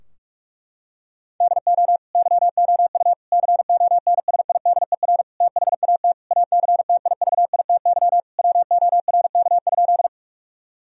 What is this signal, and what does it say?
Morse code, heard as a keyed tone, about 46 wpm, 700 Hz. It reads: DO YOU CONSIDER THAT ACTIVITY WORK?
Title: Sentences from Top 1000 Words - Review Part 4 - 45wpm